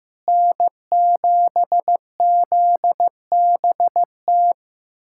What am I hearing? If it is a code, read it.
Morse code, heard as a keyed tone, about 16 wpm, 700 Hz. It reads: N7ZBT
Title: USA Callsign DE Callsign 15wpm